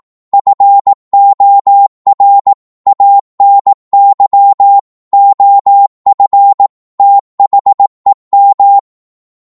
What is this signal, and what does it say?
Morse code, heard as a keyed tone, about 18 wpm, 800 Hz. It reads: FOR ANY OF THEM
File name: Morse / psg-18wpm